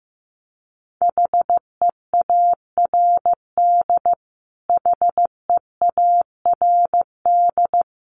Morse code, heard as a keyed tone, about 15 wpm, 700 Hz. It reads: HEARD HEARD
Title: Top 500 Words Repeated 15wpm